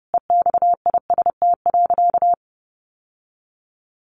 Morse code, heard as a keyed tone, about 30 wpm, 700 Hz. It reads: EXIST.
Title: Sentences from Top 1500 Words 30wpm